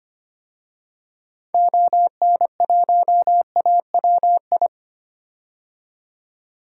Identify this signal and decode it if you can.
Morse code, heard as a keyed tone, about 25 wpm, 700 Hz. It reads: ON1AWI
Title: DXCC-100 Call Signs - 6 Characters - 25wpm